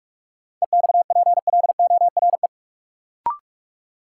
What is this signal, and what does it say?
Morse code, heard as a keyed tone, about 42 wpm, 700 Hz. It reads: EXPLORE
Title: Sentences from Top 1100 Words - New Word Drill Alt - 45wpm